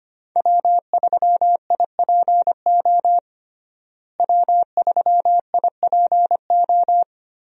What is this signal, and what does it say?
Morse code, heard as a keyed tone, about 25 wpm, 700 Hz. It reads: W3IPO W3IPO
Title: USA Call Signs - 5 Characters - Repeated 3x - 25wpm